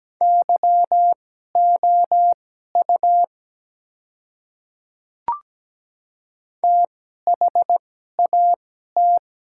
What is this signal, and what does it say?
Morse code, heard as a keyed tone, about 17 wpm, spaced as for 12 wpm, 700 Hz. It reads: YOU THAT
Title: 2x Character Spacing - Top 100 Words - 17wpm